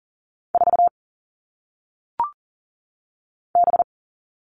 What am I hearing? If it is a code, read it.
Morse code, heard as a keyed tone, about 40 wpm, 700 Hz. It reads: 4 B
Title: All Cut Numbers - 1-Digit 40wpm